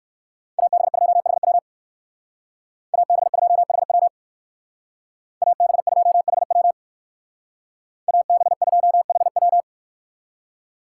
Morse code, heard as a keyed tone, about 53 wpm, 700 Hz. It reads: AB2HW AB2HW AB2HW AB2HW
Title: Speed Racing - USA 2x2 Call Signs 45wpm